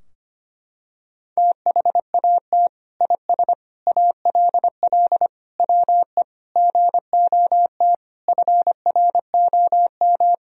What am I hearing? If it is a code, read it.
Morse code, heard as a keyed tone, about 25 wpm, 700 Hz. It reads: THAT IS ALL WE GOT FROM
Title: Sentences from Top 1000 Words - Review Part 2 - 25wpm